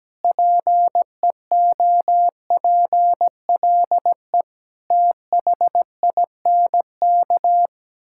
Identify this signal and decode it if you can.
Morse code, heard as a keyed tone, about 17 wpm, 700 Hz. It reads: PEOPLE THINK
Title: Sentences from Top 1000 Words - Review Part 5 - 17wpm